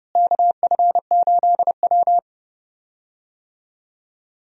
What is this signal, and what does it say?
Morse code, heard as a keyed tone, about 30 wpm, 700 Hz. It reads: KF8W
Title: USA Call Signs - 4 Characters - Repeated 3x - 30wpm